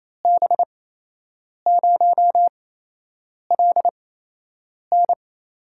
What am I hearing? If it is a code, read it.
Morse code, heard as a keyed tone, about 28 wpm, spaced as for 8 wpm, 700 Hz. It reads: B0LN
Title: USA Call Signs - 5 Characters - 8x Character Spacing - 28wpm